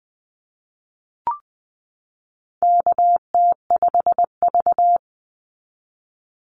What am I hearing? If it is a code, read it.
Morse code, heard as a keyed tone, about 20 wpm, 700 Hz. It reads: KT5V
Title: Call Signs Repeated 2x - 20wpm